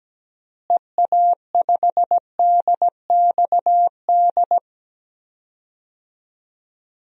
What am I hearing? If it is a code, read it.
Morse code, heard as a keyed tone, about 17 wpm, 700 Hz. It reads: EA5DXD
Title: DXCC-100 Call Signs - 6 Characters - Repeated 3x - 17wpm